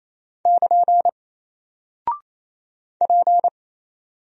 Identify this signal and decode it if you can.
Morse code, heard as a keyed tone, about 28 wpm, 700 Hz. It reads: <KN> P
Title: Single Character (Letter:Number:Punctuation:Prosign) 28wpm